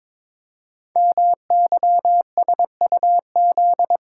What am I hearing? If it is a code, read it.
Morse code, heard as a keyed tone, about 22 wpm, 700 Hz. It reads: MYSUZ
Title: Code Groups - 5 Letters - 22wpm